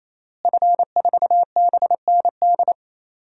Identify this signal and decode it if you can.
Morse code, heard as a keyed tone, about 27 wpm, 700 Hz. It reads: F4BND
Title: DXCC-100 Call Signs - 5 Characters - Repeated 3x - 28wpm